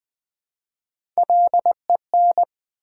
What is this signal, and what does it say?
Morse code, heard as a keyed tone, about 20 wpm, 700 Hz. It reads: LEN
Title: Build A Word - 1 Letter At A Time - 4-Letter Words - 20wpm